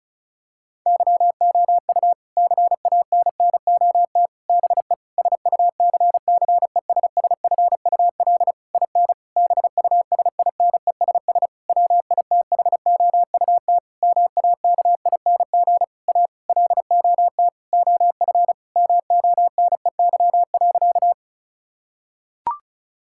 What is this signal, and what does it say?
Morse code, heard as a keyed tone, about 34 wpm, 700 Hz. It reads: YOU CANNOT BE SUCCESSFUL IN BUSINESS WITHOUT MAKING A LOT OF MONEY.
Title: Sentences from Top 1000 Words - Review Part 6 - 35wpm